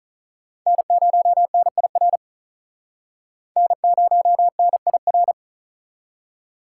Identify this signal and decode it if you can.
Morse code, heard as a keyed tone, about 39 wpm, 700 Hz. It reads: N0NIR N0NIR
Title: Speed Racing - USA Call Signs 35wpm